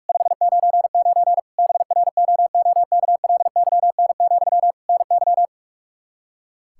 Morse code, heard as a keyed tone, about 41 wpm, 700 Hz. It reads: 599 BROOKLYN, NY
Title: Practice QSO - Exchange 1 - Beginning - 45wpm